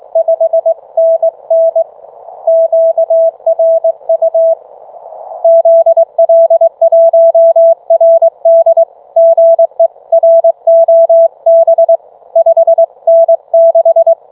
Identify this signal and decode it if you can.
Morse code, heard as a keyed tone, about 22 wpm, 660 Hz. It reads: 5NN QRU ZL1RD GE ROB 5N6